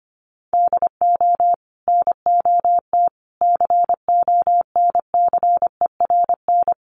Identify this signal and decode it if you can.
Morse code, heard as a keyed tone, about 25 wpm, 700 Hz. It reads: DO NOT CONCERN
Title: Sentences from Top 1000 Words 25wpm